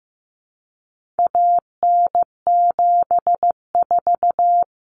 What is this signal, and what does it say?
Morse code, heard as a keyed tone, about 15 wpm, 700 Hz. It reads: AN74
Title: Common Cut Numbers - 4-Digit 15wpm